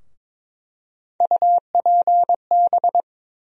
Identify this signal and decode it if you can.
Morse code, heard as a keyed tone, about 21 wpm, 700 Hz. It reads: UPB
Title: USA Call Sign Postfixes - 3 Letters 22wpm